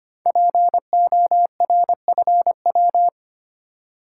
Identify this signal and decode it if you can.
Morse code, heard as a keyed tone, about 25 wpm, 700 Hz. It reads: PORFW
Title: Code Groups - 5 Letters - 25wpm